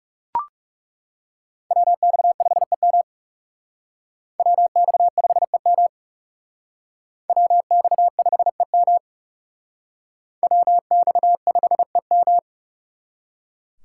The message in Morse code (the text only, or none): WX5EM WX5EM WX5EM WX5EM